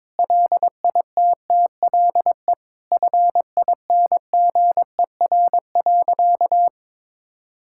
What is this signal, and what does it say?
Morse code, heard as a keyed tone, about 22 wpm, 700 Hz. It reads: LITTLE FINGER.
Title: Sentences from Top 1600 Words 22wpm